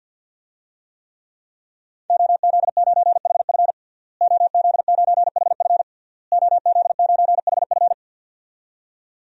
Morse code, heard as a keyed tone, about 51 wpm, 700 Hz. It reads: OZ9HF OZ9HF OZ9HF
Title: DXCC-100 Call Signs - 5 Characters - Repeated 3x - 50wpm